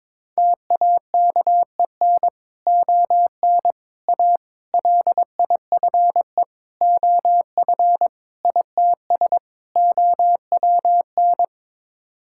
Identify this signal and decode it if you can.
Morse code, heard as a keyed tone, about 22 wpm, 700 Hz. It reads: TAKEN ON A LIFE OF ITS OWN
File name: practice / 22wpm / Sets of 7 Words - 22wpm